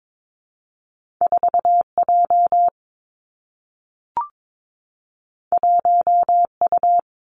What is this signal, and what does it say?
Morse code, heard as a keyed tone, about 22 wpm, 700 Hz. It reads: 4 J 1 U
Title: Two Challenging Letter-Number Combinations 22wpm